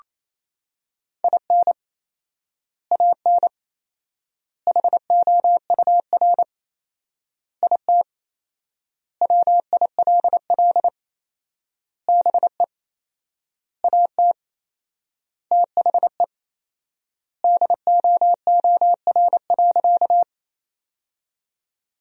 Morse code, heard as a keyed tone, about 28 wpm, 700 Hz. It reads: IN AN HOUR IT WILL BE AT THE DOOR.
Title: 4x Word-Spacing - Sentences from Top 400 Words - 28wpm